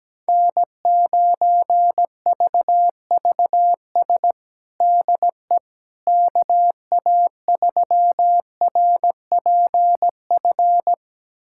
Morse code, heard as a keyed tone, about 17 wpm, 700 Hz. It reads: N9VVS DE KA3RPF
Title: USA Callsign DE Callsign 17wpm